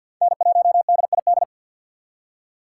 Morse code, heard as a keyed tone, about 47 wpm, 700 Hz. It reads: N1DID
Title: K1USN SST QSO Practice - 50wpm